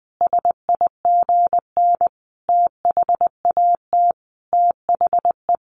Morse code, heard as a keyed tone, about 20 wpm, 700 Hz. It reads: SIGN THAT THE
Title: Sentences from Top 1400 Words 20wpm